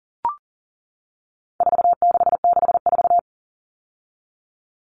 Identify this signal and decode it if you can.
Morse code, heard as a keyed tone, about 40 wpm, 700 Hz. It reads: 4664
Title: All Cut Numbers - 4-Digit 40wpm